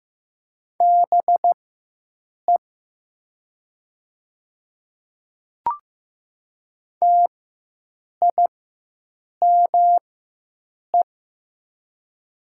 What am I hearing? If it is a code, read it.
Morse code, heard as a keyed tone, about 15 wpm, spaced as for 7 wpm, 700 Hz. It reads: BE TIME
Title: 4x Character Spacing - Top 300 Words - 15wpm